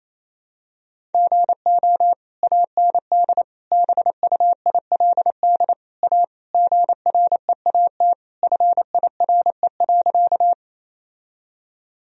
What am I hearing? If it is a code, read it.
Morse code, heard as a keyed tone, about 27 wpm, 700 Hz. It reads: GO AND BUILD A GREAT FIRE.
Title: Sentences from Top 1000 Words - Review Part 1 - 28wpm